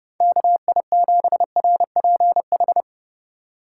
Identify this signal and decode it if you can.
Morse code, heard as a keyed tone, about 30 wpm, 700 Hz. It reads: KI7RPH
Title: USA Call Signs - 6 Characters - 2x Character Spacing - 30wpm